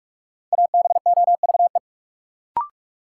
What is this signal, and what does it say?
Morse code, heard as a keyed tone, about 41 wpm, 700 Hz. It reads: ABOVE
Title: Sentences from Top 1200 Words - New Word Drill Alt - 45wpm